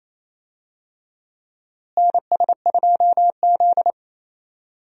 Morse code, heard as a keyed tone, about 28 wpm, 700 Hz. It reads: NS2Z
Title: USA Call Signs - 4 Characters - Repeated 2x - 28wpm